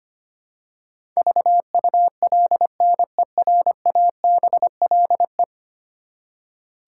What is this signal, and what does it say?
Morse code, heard as a keyed tone, about 25 wpm, 700 Hz. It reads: VULNERABLE
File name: Ten Letter Words - 25wpm